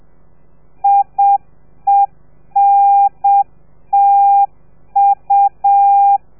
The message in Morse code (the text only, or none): IENTU